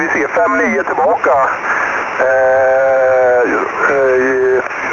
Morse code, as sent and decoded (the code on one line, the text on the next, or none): none